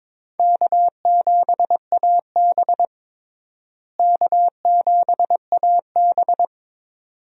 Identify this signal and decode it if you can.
Morse code, heard as a keyed tone, about 21 wpm, 700 Hz. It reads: K7AB K7AB
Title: USA Call Signs - 4 Characters - Repeated 2x - 22wpm